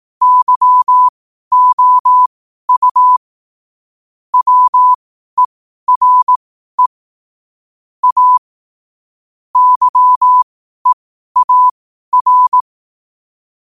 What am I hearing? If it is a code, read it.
Morse code, heard as a keyed tone, about 18 wpm, spaced as for 12 wpm, 1000 Hz. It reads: YOU WERE A YEAR
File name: qotdfiles / 12wpm